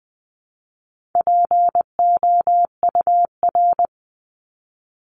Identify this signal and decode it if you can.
Morse code, heard as a keyed tone, about 20 wpm, 700 Hz. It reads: POUR
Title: 260 Lesson 42e - Words 20-20wpm